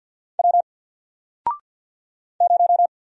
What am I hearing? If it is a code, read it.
Morse code, heard as a keyed tone, about 54 wpm, 700 Hz. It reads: W 0
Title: Single Letter-Number 50wpm